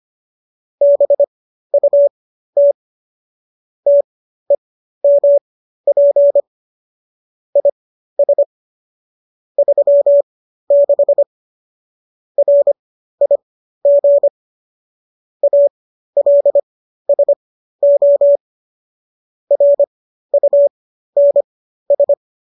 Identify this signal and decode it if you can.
Morse code, heard as a keyed tone, about 25 wpm, spaced as for 13 wpm, 570 Hz. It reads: BUT TEMP IS 36 RIG ALSO RUNS